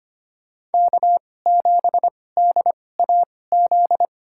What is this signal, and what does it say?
Morse code, heard as a keyed tone, about 25 wpm, spaced as for 18 wpm, 700 Hz. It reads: K7DAZ
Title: USA Call Signs - 5 Characters - 2x Character Spacing - 25wpm